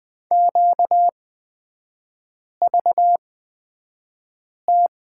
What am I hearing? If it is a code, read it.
Morse code, heard as a keyed tone, about 20 wpm, spaced as for 5 wpm, 700 Hz. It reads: QVT
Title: 250 Lesson 40e - BK - Three Characters 20wpm